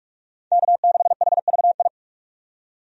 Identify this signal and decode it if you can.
Morse code, heard as a keyed tone, about 47 wpm, 700 Hz. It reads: K6HVI
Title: USA Call Signs - 5 Characters - 45wpm